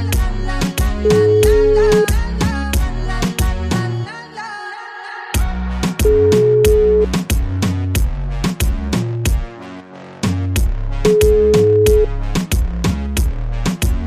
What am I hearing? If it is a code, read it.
Ringback tone (Europe, 425 Hz).